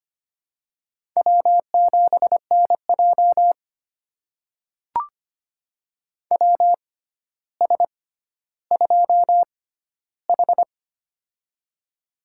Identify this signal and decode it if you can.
Morse code, heard as a keyed tone, about 25 wpm, spaced as for 13 wpm, 700 Hz. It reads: W7NJ WS2H
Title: USA Call Signs - 4 Characters - 6x Character Spacing - 25wpm